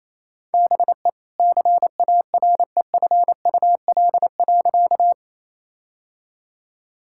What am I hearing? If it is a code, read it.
Morse code, heard as a keyed tone, about 28 wpm, 700 Hz. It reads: BE CAREFUL.
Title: Sentences from Top 1600 Words 28wpm